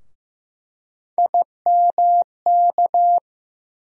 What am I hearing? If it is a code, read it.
Morse code, heard as a keyed tone, about 15 wpm, 700 Hz. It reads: IMK
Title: USA Call Sign Postfixes - 3 Letters 15wpm